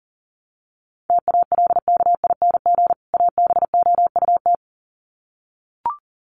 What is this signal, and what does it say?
Morse code, heard as a keyed tone, about 40 wpm, 700 Hz. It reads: TALKING ABOUT
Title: Sets of 2 Words 40wpm